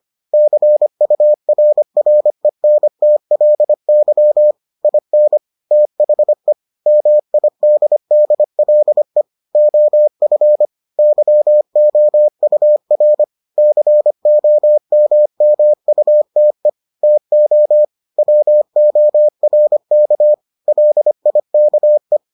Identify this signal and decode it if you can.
Morse code, heard as a keyed tone, about 25 wpm, 600 Hz. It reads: CURRENTLY IN THE MIDDLE OF YOUR COMMUTE TO WORK LIKE